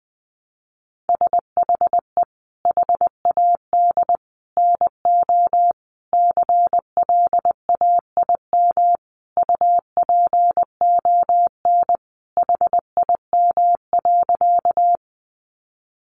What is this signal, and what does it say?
Morse code, heard as a keyed tone, about 19 wpm, 700 Hz. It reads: SHE HAD NO CLAIM UPON HIM.
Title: Sentences from Top 1500 Words 20wpm